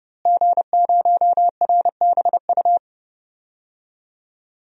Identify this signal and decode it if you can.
Morse code, heard as a keyed tone, about 31 wpm, 700 Hz. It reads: G0RBU
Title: DXCC-100 Call Signs - 5 Characters - Repeated 3x - 30wpm